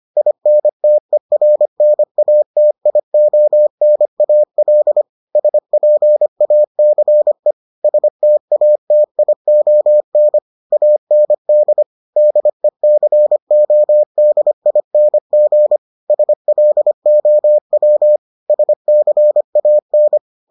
Morse code, heard as a keyed tone, about 25 wpm, 600 Hz. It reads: INTERNATIONAL SPACE STATION AND DECODING SLOW SCAN